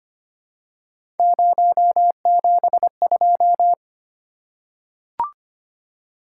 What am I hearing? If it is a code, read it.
Morse code, heard as a keyed tone, about 25 wpm, 700 Hz. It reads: 072
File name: Three Letter-Number Combinations - 25wpm